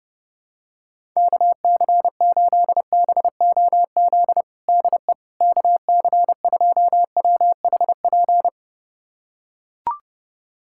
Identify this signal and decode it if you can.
Morse code, heard as a keyed tone, about 30 wpm, 700 Hz. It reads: KC8BOZ DE KC2WHP
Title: USA Callsign DE Callsign - 6 Characters - 30wpm